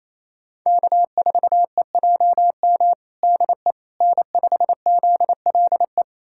Morse code, heard as a keyed tone, about 28 wpm, 700 Hz. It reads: K4EJM DE N5ZLE
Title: USA Callsign DE Callsign - 5 Characters - 28wpm